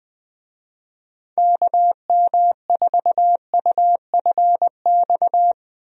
Morse code, heard as a keyed tone, about 21 wpm, 700 Hz. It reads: KM4UFX
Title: USA Call Signs - 6 Characters - Repeated 2x - 20wpm